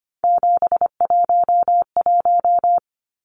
Morse code, heard as a keyed tone, about 24 wpm, 700 Hz. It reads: 711
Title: Numbers - 3-Digit 25wpm